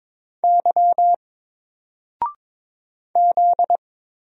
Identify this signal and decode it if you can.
Morse code, heard as a keyed tone, about 22 wpm, 700 Hz. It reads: Y Z
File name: Single-Letter-Number-22wpm